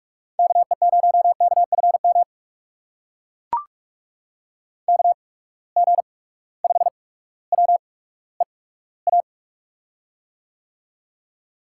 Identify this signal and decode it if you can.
Morse code, heard as a keyed tone, about 44 wpm, spaced as for 19 wpm, 700 Hz. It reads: KE0KFM KG5WEA
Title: USA Call Signs - 6 Characters - 8x Character Spacing - 45wpm